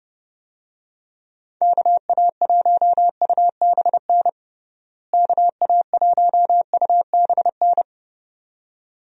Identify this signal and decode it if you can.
Morse code, heard as a keyed tone, about 30 wpm, 700 Hz. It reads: KA1UBN KA1UBN
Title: USA Call Signs - 6 Characters - Repeated 2x - 30wpm